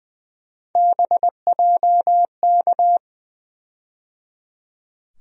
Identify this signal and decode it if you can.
Morse code, heard as a keyed tone, about 20 wpm, 700 Hz. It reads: BJK
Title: USA Call Sign Postfixes - 3 Letters 20wpm